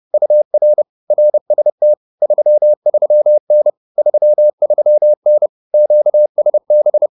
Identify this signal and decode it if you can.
Morse code, heard as a keyed tone, about 30 wpm, 600 Hz. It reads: UR RST 33N 33N QSB